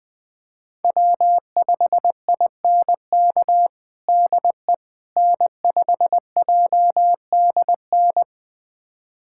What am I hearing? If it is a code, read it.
Morse code, heard as a keyed tone, about 21 wpm, 700 Hz. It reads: W5INK DE N5JDN
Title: USA Callsign DE Callsign - 5 Characters - 20wpm